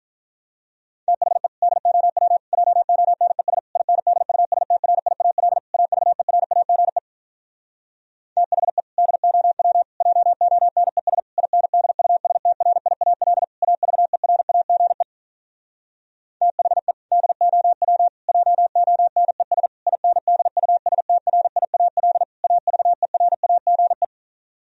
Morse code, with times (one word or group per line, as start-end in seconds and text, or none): 1.1-1.5 THE
1.6-2.4 DOW
2.5-3.6 JONES
3.8-5.6 INDUSTRIAL
5.7-7.0 AVERAGE
8.4-8.8 THE
9.0-9.8 DOW
10.0-11.2 JONES
11.4-13.4 INDUSTRIAL
13.6-15.0 AVERAGE
16.4-16.9 THE
17.1-18.1 DOW
18.3-19.7 JONES
19.9-22.2 INDUSTRIAL
22.4-24.0 AVERAGE